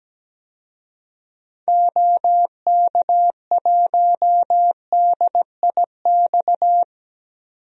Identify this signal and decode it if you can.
Morse code, heard as a keyed tone, about 17 wpm, 700 Hz. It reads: OK1DIX
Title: DXCC-100 Call Signs - 6 Characters - 17wpm